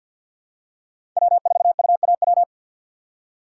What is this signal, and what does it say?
Morse code, heard as a keyed tone, about 51 wpm, 700 Hz. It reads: W4UAW
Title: USA Call Signs - 5 Characters - 4x Character Spacing - 50wpm